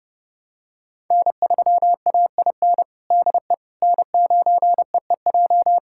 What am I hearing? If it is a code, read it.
Morse code, heard as a keyed tone, about 30 wpm, 700 Hz. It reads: N3AIN DE N9EEJ
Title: USA Callsign DE Callsign 30wpm